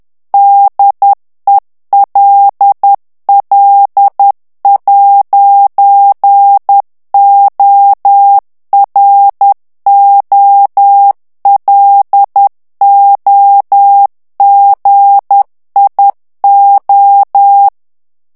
Morse code, heard as a keyed tone, about 10 wpm, 800 Hz. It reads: DELL'OROLOGIO
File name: ITALIEN texte clair 1 vitesse 10 mots minute